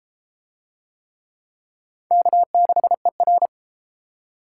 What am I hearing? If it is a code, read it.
Morse code, heard as a keyed tone, about 31 wpm, 700 Hz. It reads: K6ER